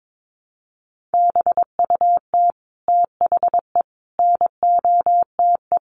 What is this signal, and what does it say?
Morse code, heard as a keyed tone, about 22 wpm, 700 Hz. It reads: BUT THE NOTE
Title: Sentences from Top 800 Words 22wpm